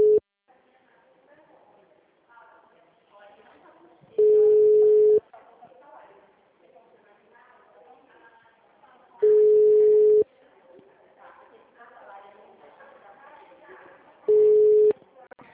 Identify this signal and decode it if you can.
Ringback tone (Europe, 425 Hz), with speech.
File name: out- -2007- - - .99055